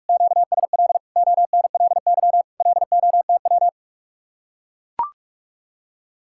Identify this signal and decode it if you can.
Morse code, heard as a keyed tone, about 44 wpm, 700 Hz. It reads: QSL ONLY LOTW